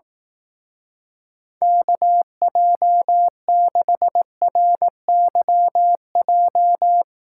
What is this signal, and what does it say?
Morse code, heard as a keyed tone, about 18 wpm, 700 Hz. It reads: KJ6RYJ
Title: Speed Racing - USA 2x3 Call Signs 15wpm